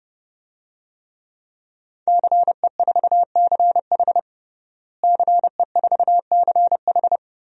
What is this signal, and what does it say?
Morse code, heard as a keyed tone, about 32 wpm, 700 Hz. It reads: CE4CH CE4CH
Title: DXCC-100 Call Signs - 5 Characters - Repeated 2x - 30wpm